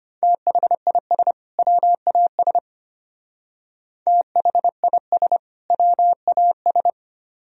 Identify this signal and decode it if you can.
Morse code, heard as a keyed tone, about 33 wpm, 700 Hz. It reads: THIS WAS THIS WAS